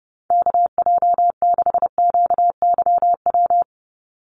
Morse code, heard as a keyed tone, about 30 wpm, 700 Hz. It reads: KJ6QYW
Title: USA Call Signs 30wpm